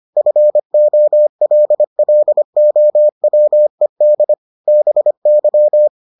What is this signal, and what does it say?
Morse code, heard as a keyed tone, about 25 wpm, 600 Hz. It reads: FOLLOWED BY